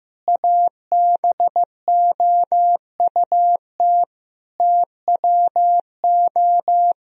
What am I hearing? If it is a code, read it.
Morse code, heard as a keyed tone, about 16 wpm, 700 Hz. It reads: ABOUT TWO
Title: Sentences from Top 1000 Words - Review Part 3 - 15wpm